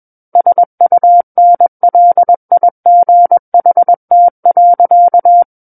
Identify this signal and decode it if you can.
Morse code, heard as a keyed tone, about 21 wpm, 700 Hz. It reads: SUNLIGHT.
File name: Book One / 12-Chapter 12 - 21 WPM